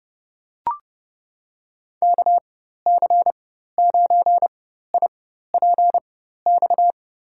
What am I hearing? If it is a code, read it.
Morse code, heard as a keyed tone, about 30 wpm, spaced as for 14 wpm, 700 Hz. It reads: KC9IPX
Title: USA Call Signs - 6 Characters - 4x Character Spacing - 30wpm